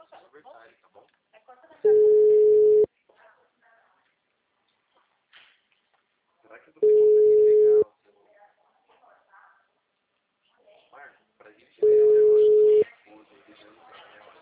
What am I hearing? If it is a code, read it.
Ringback tone (Europe, 425 Hz), with speech.